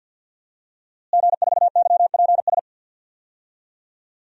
Morse code, heard as a keyed tone, about 52 wpm, 700 Hz. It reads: G4YPS
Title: DXCC-100 Call Signs - 5 Characters - Repeated 2x - 50wpm